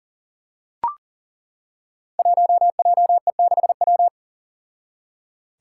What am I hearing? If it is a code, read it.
Morse code, heard as a keyed tone, about 40 wpm, 700 Hz. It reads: 1JE6W
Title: Five Challenging Letter-Number Combinations 40wpm